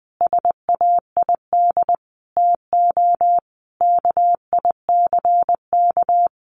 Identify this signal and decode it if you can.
Morse code, heard as a keyed tone, about 20 wpm, 700 Hz. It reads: SAID TO KICK
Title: Sentences from Top 1300 Words 20wpm